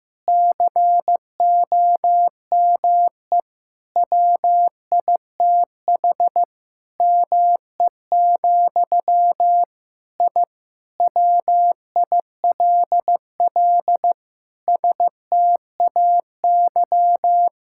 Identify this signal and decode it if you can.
Morse code, heard as a keyed tone, about 15 wpm, 700 Hz. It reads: COME WITH ME, I WILL STAY
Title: Sentences from Top 1000 Words - Review Part 6 - 15wpm